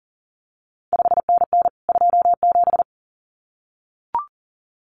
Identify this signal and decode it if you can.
Morse code, heard as a keyed tone, about 38 wpm, 700 Hz. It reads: 5NN 27
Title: CQ WW DX Contest Exchange Encore - 40wpm